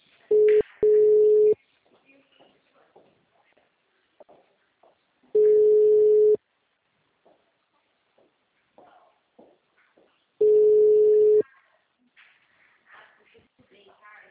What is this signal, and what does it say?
Ringback tone (Europe, 425 Hz), with speech.